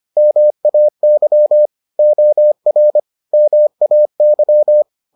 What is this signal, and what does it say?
Morse code, heard as a keyed tone, about 25 wpm, 600 Hz. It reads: MAY OR MAY